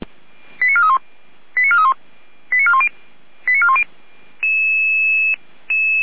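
Five-tone selective call (ZVEI), heard as quick runs of five tones, tones of about 70 ms. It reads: 89431 89431 89410 89410
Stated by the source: ZVEI five-tone selective call 89457 received with an SDR, logged by the Orlamünde fire brigade (feuerwehr-orlamuende).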